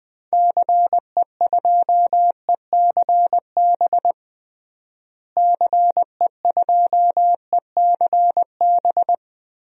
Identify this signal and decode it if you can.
Morse code, heard as a keyed tone, about 20 wpm, 700 Hz. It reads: CE2ECB CE2ECB
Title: DXCC-100 Call Signs - 6 Characters - Repeated 3x - 20wpm